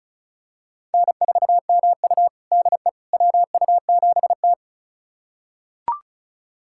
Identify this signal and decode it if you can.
Morse code, heard as a keyed tone, about 36 wpm, 700 Hz. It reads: N4MU DE WU7T
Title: USA Callsign DE Callsign - 4 Characters - 35wpm